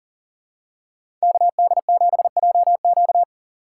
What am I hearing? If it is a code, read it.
Morse code, heard as a keyed tone, about 40 wpm, 700 Hz. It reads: KD7JQ
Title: USA Call Signs - 5 Characters - 2x Character Spacing - 40wpm